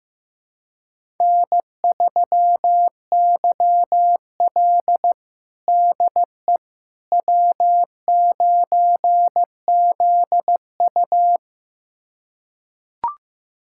Morse code, heard as a keyed tone, about 16 wpm, 700 Hz. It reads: N3YL DE W9ZU
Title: USA Callsign DE Callsign - 4 Characters - 15wpm